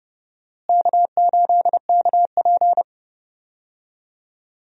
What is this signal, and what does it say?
Morse code, heard as a keyed tone, about 30 wpm, 700 Hz. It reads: K8KP
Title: USA Call Signs - 4 Characters - Repeated 3x - 30wpm